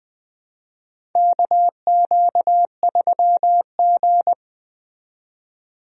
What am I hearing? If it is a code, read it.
Morse code, heard as a keyed tone, about 20 wpm, 700 Hz. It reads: KQ3G
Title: USA Call Signs - 4 Characters - 20wpm